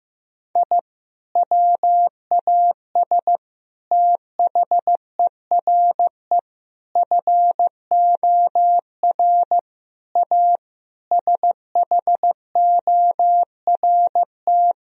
Morse code, heard as a keyed tone, about 15 wpm, 700 Hz. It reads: I WAS THERE FOR A SHORT
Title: Sentences from Top 1000 Words - Review Part 2 - 15wpm